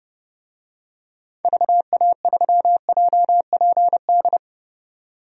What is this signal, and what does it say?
Morse code, heard as a keyed tone, about 30 wpm, 700 Hz. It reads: VA3JPD
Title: DXCC-100 Call Signs - 6 Characters - 30wpm